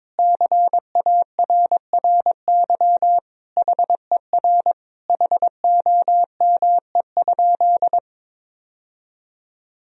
Morse code, heard as a keyed tone, about 22 wpm, 700 Hz. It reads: CARRY HER HOME?
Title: Sentences from Top 1000 Words - Review Part 4 - 22wpm